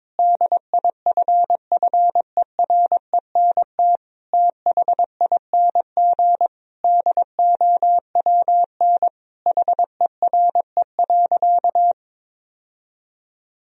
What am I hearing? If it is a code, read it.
Morse code, heard as a keyed tone, about 22 wpm, 700 Hz. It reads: DIFFERENT THING DOWN HERE.
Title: Sentences from Top 1000 Words - Review Part 3 - 22wpm